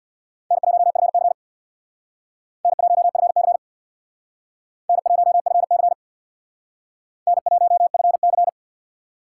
Morse code, heard as a keyed tone, about 64 wpm, 700 Hz. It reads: N1FC N1FC N1FC N1FC